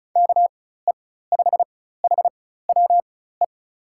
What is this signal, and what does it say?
Morse code, heard as a keyed tone, about 36 wpm, spaced as for 16 wpm, 700 Hz. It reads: KE5HWE
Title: USA Call Signs - 6 Characters - 4x Character Spacing - 35wpm